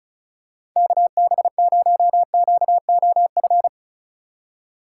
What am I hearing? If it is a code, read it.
Morse code, heard as a keyed tone, about 36 wpm, 700 Hz. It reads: KB0QOF
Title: USA Call Signs - 6 Characters - 2x Character Spacing - 35wpm